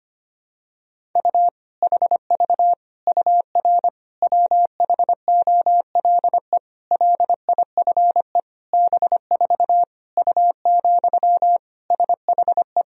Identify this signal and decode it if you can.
Morse code, heard as a keyed tone, about 25 wpm, 700 Hz. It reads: U HV UR WHOLE LIFE B4 U, SHE
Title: Sentences from Top 400 Words and Abbreviations 25wpm